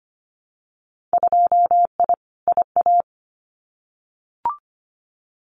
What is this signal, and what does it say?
Morse code, heard as a keyed tone, about 25 wpm, 700 Hz. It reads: 2I IA
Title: Winter Field Day Exchanges Encore 25wpm